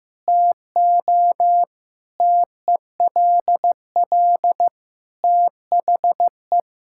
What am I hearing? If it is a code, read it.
Morse code, heard as a keyed tone, about 15 wpm, 700 Hz. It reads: TO TELL THE
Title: Sentences from Top 1000 Words - Review Part 6 - 15wpm